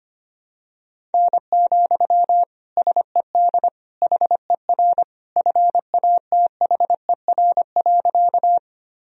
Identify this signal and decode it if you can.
Morse code, heard as a keyed tone, about 25 wpm, 700 Hz. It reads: N, SED HER FATHER.
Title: Sentences from Top 300 Words and Abbreviations 25wpm